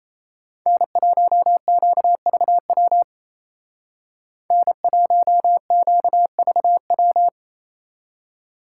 Morse code, heard as a keyed tone, about 29 wpm, 700 Hz. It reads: N1QVW N1QVW
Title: Speed Racing - USA 1x3 Call Signs 28wpm